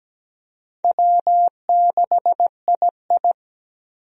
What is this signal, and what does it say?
Morse code, heard as a keyed tone, about 17 wpm, 700 Hz. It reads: W6II
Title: USA Call Signs - 4 Characters - 17wpm